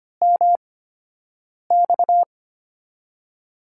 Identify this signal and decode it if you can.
Morse code, heard as a keyed tone, about 25 wpm, spaced as for 7 wpm, 700 Hz. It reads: MX
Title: USA Call Signs - 5 Characters - 8x Character Spacing - 25wpm